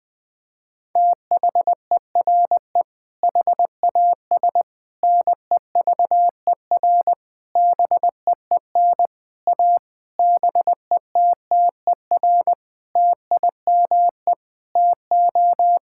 Morse code, heard as a keyed tone, about 19 wpm, 700 Hz. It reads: THERE HAS NEVER BEEN A BETTER TIME TO
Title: Sentences with 8-Word Phrases - Spoken Beginning - 20wpm